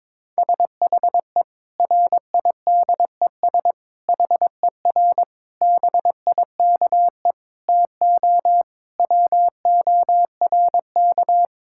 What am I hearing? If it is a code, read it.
Morse code, heard as a keyed tone, about 22 wpm, 700 Hz. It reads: SHE RIDES HER BIKE TO WORK
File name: Difficult Sentences with Hints - 22wpm